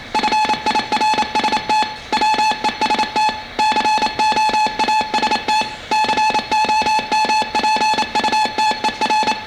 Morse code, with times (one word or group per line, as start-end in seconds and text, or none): 0.2-1.8 FIRST
2.1-3.3 WEST
3.6-5.6 COAST
5.9-9.3 COMPUTER